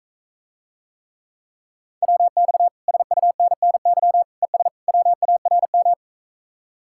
Morse code, heard as a keyed tone, about 45 wpm, 700 Hz. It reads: WX SUNNY ES WARM